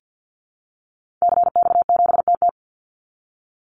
Morse code, heard as a keyed tone, about 50 wpm, 700 Hz. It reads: CX7TT
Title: Call Signs Repeated 3x - 50wpm